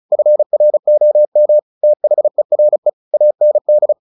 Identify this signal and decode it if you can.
Morse code, heard as a keyed tone, about 35 wpm, 600 Hz. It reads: FROM THERE AND
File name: the old man and the sea 2 35wpm